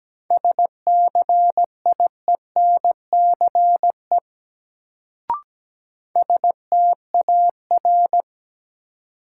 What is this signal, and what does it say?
Morse code, heard as a keyed tone, about 17 wpm, 700 Hz. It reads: SCIENCE STAR
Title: Sentences from Top 1000 Words - Review Part 2 - New Word Drill - 17wpm